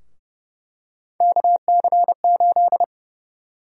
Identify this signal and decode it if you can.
Morse code, heard as a keyed tone, about 30 wpm, 700 Hz. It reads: KC8
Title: USA Call Sign Prefixes - 30wpm